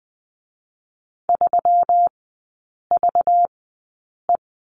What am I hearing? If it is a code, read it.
Morse code, heard as a keyed tone, about 20 wpm, spaced as for 8 wpm, 700 Hz. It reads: 3VE
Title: 191 Lesson 31e - Letter v - Three Characters 20wpm